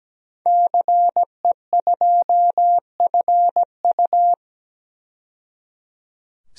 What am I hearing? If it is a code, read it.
Morse code, heard as a keyed tone, about 17 wpm, 700 Hz. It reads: CE2FU
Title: DXCC-100 Call Signs - 5 Characters - 17wpm